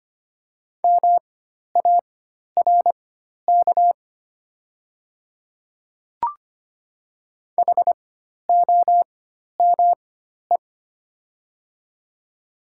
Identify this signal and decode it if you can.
Morse code, heard as a keyed tone, about 25 wpm, spaced as for 12 wpm, 700 Hz. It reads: MARK HOME
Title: 4x Character Spacing - Top 300 Words - 25wpm